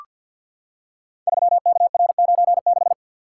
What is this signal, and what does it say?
Morse code, heard as a keyed tone, about 47 wpm, 700 Hz. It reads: 3KR96
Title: Code Groups - 5 Letter:Numbers - 50wpm